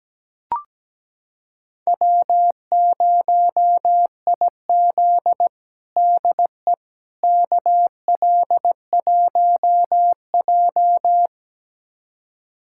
Morse code, heard as a keyed tone, about 17 wpm, 700 Hz. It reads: W0IZ DE KL1J
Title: USA Callsign DE Callsign - 4 Characters - 17wpm